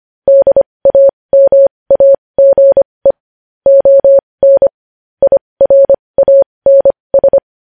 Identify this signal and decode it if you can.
Morse code, heard as a keyed tone, about 25 wpm, spaced as for 20 wpm, 550 Hz. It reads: DAMAGE ON IRANS
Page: News headlines in Morse code at 20 wpm